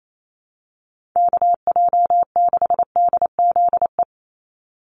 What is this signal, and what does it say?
Morse code, heard as a keyed tone, about 28 wpm, 700 Hz. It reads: KJ6DZE
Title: USA Call Signs 28wpm